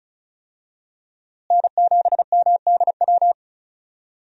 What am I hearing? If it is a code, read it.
Morse code, heard as a keyed tone, about 36 wpm, 700 Hz. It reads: N7MDW
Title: USA Call Signs - 5 Characters - Repeated 2x - 35wpm